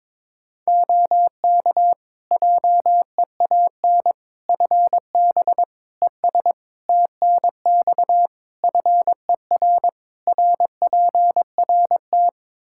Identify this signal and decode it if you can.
Morse code, heard as a keyed tone, about 22 wpm, 700 Hz. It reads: OK JEAN FB ES TNX FER RPRT
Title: Practice QSO - Exchange 2 - Info - 22wpm